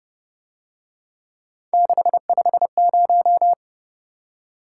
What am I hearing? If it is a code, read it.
Morse code, heard as a keyed tone, about 29 wpm, 700 Hz. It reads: 650
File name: Speed-Racing - Three Letter-Number Combinations - 25wpm